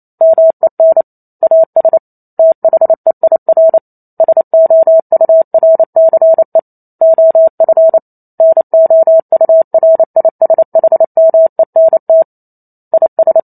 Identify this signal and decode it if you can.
Morse code, heard as a keyed tone, about 28 wpm, 650 Hz. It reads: MEN AS THEIR SOURCE OF NOURISHMENT IS